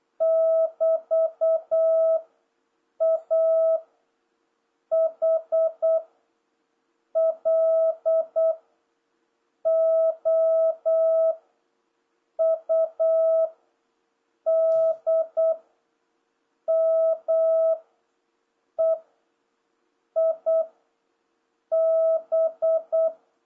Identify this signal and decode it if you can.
Morse code, heard as a keyed tone, about 8 wpm, 650 Hz. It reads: <BT> A H L O U D M E I B